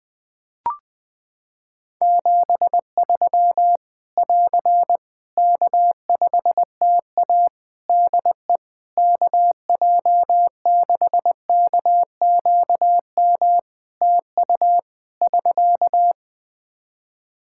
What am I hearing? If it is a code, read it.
Morse code, heard as a keyed tone, about 20 wpm, 700 Hz. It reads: 73 <AR> K5TA DE KJ6KQM TU <SK>
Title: Practice QSO - Exchange 4 - Ending - 20wpm